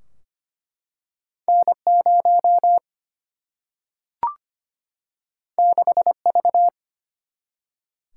Morse code, heard as a keyed tone, about 25 wpm, 700 Hz. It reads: N0 6V
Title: Single-Letter Single-Number 25wpm